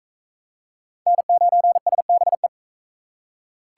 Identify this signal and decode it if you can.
Morse code, heard as a keyed tone, about 38 wpm, 700 Hz. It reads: N9SBE